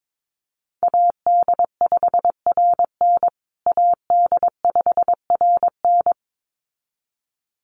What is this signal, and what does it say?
Morse code, heard as a keyed tone, about 22 wpm, 700 Hz. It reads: AD5RN AD5RN
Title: USA Call Signs Repeated 2x 22wpm